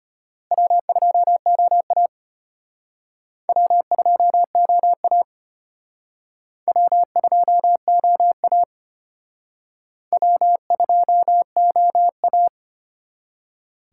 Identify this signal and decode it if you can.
Morse code, heard as a keyed tone, about 33 wpm, 700 Hz. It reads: W2OA W2OA W2OA W2OA